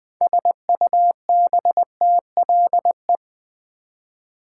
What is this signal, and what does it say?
Morse code, heard as a keyed tone, about 20 wpm, 700 Hz. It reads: SUBTLE
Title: 6 Letter Words - First 3 Letters Given - 20wpm